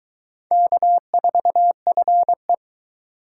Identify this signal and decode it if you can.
Morse code, heard as a keyed tone, about 23 wpm, 700 Hz. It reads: K4FE